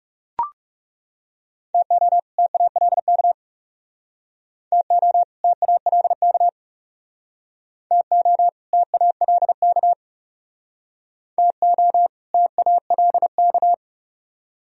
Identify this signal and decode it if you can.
Morse code, heard as a keyed tone, about 37 wpm, 700 Hz. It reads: TO TALK TO TALK TO TALK TO TALK
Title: Speed-Racing - Sets of 2 Words 30wpm